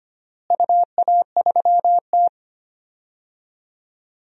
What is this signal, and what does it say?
Morse code, heard as a keyed tone, about 25 wpm, 700 Hz. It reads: UA3T
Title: DXCC-100 Call Signs - 4 Characters - Repeated 3x - 25wpm